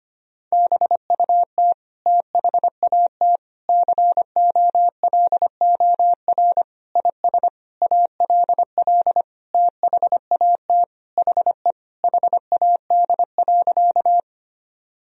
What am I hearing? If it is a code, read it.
Morse code, heard as a keyed tone, about 25 wpm, 700 Hz. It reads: BUT THAT COLOR IS ALL THAT HE HAD.
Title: Sentences from Top 1000 Words - Review Part 2 - 25wpm